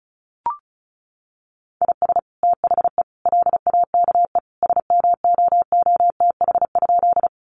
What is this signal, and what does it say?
Morse code, heard as a keyed tone, about 34 wpm, 700 Hz. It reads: IS THE LAKE SMOOTH?
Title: Sentences from Top 1400 Words 35wpm